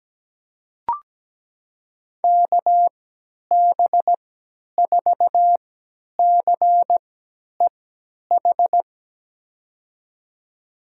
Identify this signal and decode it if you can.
Morse code, heard as a keyed tone, about 17 wpm, spaced as for 10 wpm, 700 Hz. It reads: KB4CEH
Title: USA Call Signs - 6 Characters - 3x Character Spacing - 17wpm